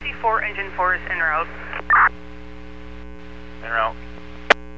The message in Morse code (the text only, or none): none